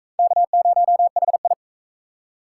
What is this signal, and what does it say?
Morse code, heard as a keyed tone, about 40 wpm, 700 Hz. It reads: K0HI